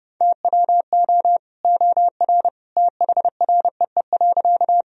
Morse code, heard as a keyed tone, about 32 wpm, 700 Hz. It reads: TWO OR THREE.
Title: Sentences from Top 1000 Words - Review Part 1 - 30wpm